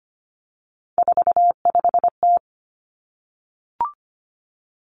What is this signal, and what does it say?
Morse code, heard as a keyed tone, about 25 wpm, 700 Hz. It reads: 45T
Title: Three Challenging Letter-Number Combinations 25wpm